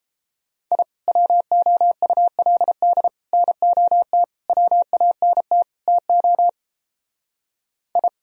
Morse code, heard as a keyed tone, about 34 wpm, 700 Hz. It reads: I WOULD NOT WANT TO I
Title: Speed-Racing - Sets of 5 Words 28wpm